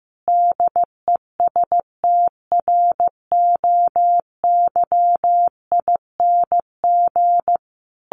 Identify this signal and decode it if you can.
Morse code, heard as a keyed tone, about 15 wpm, 700 Hz. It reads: DESTROYING
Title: A Sound Of Thunder 15wpm